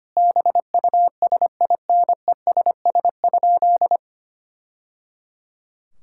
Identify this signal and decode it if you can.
Morse code, heard as a keyed tone, about 25 wpm, 700 Hz. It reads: BUSINESS?
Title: Sentences from Top 1000 Words - Review Part 5 - 25wpm